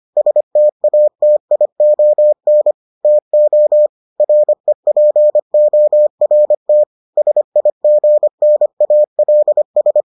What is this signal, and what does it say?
Morse code, heard as a keyed tone, about 25 wpm, 600 Hz. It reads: STATION TO REPORT SIGNALS